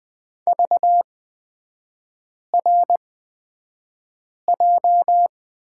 Morse code, heard as a keyed tone, about 20 wpm, spaced as for 5 wpm, 690 Hz. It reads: VRJ
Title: 229 Lesson 37e - Letter x - Three Characters 20wpm